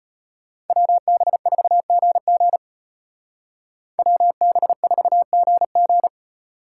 Morse code, heard as a keyed tone, about 32 wpm, 700 Hz. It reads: WB4GG WB4GG